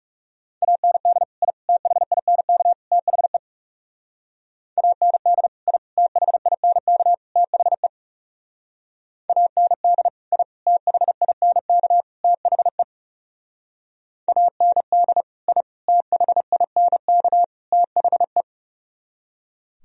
Morse code, heard as a keyed tone, about 38 wpm, 690 Hz. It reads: AND I THINK THE AND I THINK THE AND I THINK THE AND I THINK THE